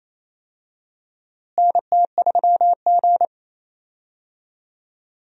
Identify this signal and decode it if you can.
Morse code, heard as a keyed tone, about 28 wpm, 700 Hz. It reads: NT3G
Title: USA Call Signs - 4 Characters - Repeated 2x - 28wpm